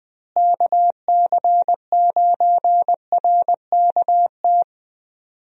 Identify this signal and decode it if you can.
Morse code, heard as a keyed tone, about 21 wpm, 700 Hz. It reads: KC9RKT
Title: USA Call Signs - 6 Characters - Repeated 2x - 20wpm